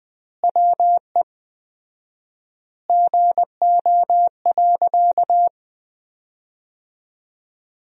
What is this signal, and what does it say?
Morse code, heard as a keyed tone, about 20 wpm, 700 Hz. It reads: WE GO.
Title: 4x Word-Spacing - Sentences from Top 200 Words - 20wpm